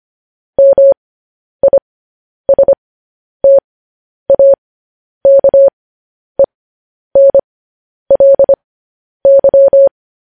Morse code, heard as a keyed tone, about 25 wpm, spaced as for 10 wpm, 550 Hz. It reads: MISTAKENLY
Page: News Headlines in Morse Code at 10 WPM